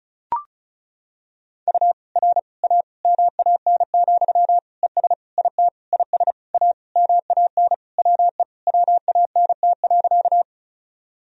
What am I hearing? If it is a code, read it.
Morse code, heard as a keyed tone, about 36 wpm, 700 Hz. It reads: U R A MAN, ES IT IS A MAN WE WANT.
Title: Sentences from Top 200 Words and Abbreviations 35wpm